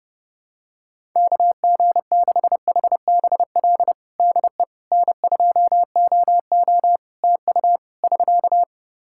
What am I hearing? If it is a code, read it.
Morse code, heard as a keyed tone, about 30 wpm, 700 Hz. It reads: KG6HBL DE N2OO TU <SK>
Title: K1USN SST QSO Practice 30wpm